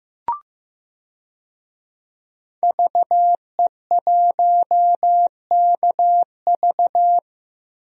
Morse code, heard as a keyed tone, about 15 wpm, 700 Hz. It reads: VE1KV
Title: DXCC-100 Call Signs - 5 Characters - Repeated 2x - 15wpm